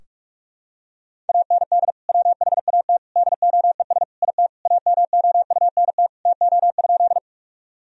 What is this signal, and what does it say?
Morse code, heard as a keyed tone, about 44 wpm, 700 Hz. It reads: AND WHAT DOES IT AMOUNT TO?
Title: Sentences from Top 1000 Words - Review Part 4 - 45wpm